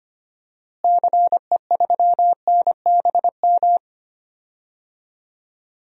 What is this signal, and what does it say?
Morse code, heard as a keyed tone, about 25 wpm, 700 Hz. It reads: CE3NBM
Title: DXCC-100 Call Signs - 6 Characters - Repeated 3x - 25wpm